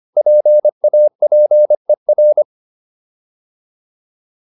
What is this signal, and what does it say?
Morse code, heard as a keyed tone, about 25 wpm, 600 Hz. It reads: PAPER